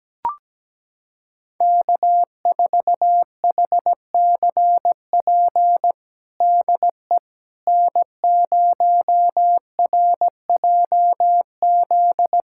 Morse code, heard as a keyed tone, about 17 wpm, 700 Hz. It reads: K4HCP DE N0RJZ
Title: USA Callsign DE Callsign - 5 Characters - 17wpm